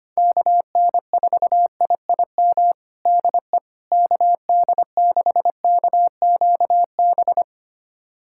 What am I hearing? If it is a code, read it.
Morse code, heard as a keyed tone, about 25 wpm, 700 Hz. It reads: KN4IIM DE KD6KQB
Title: USA Callsign DE Callsign 25wpm